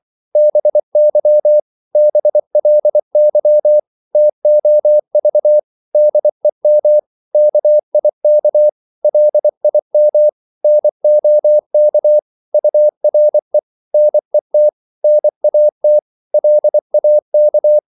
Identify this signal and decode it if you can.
Morse code, heard as a keyed tone, about 24 wpm, 600 Hz. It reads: BY BLY TOV DEM KIK LIM NOK URE NET NAT LAK
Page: Korte danske ord 24wpm